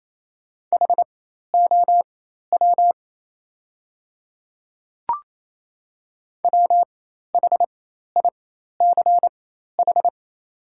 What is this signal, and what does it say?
Morse code, heard as a keyed tone, about 28 wpm, spaced as for 13 wpm, 700 Hz. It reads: HOW WHICH
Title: 4x Character Spacing - Top 100 Words - 28wpm